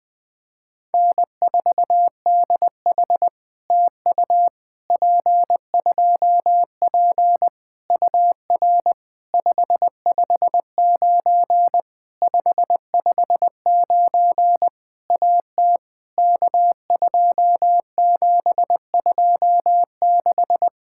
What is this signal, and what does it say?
Morse code, heard as a keyed tone, about 20 wpm, 700 Hz. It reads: N4DH TU P2P UR 559 559 AT K2726
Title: POTA QSO Practice - 20wpm